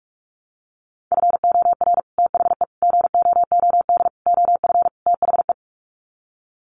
Morse code, heard as a keyed tone, about 44 wpm, 700 Hz. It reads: FOR THE GOOD OF THE
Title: Sets of 5 Words Encore2 45wpm